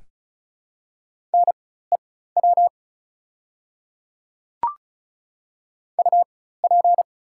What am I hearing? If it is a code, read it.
Morse code, heard as a keyed tone, about 36 wpm, spaced as for 16 wpm, 690 Hz. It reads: NEW UP
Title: 4x Character Spacing - Top 500 Words - 35wpm